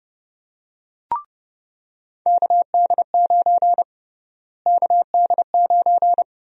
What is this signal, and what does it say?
Morse code, heard as a keyed tone, about 32 wpm, 700 Hz. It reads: KD9 KD9
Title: USA Call Sign Prefixes - Repeated 2x - 30wpm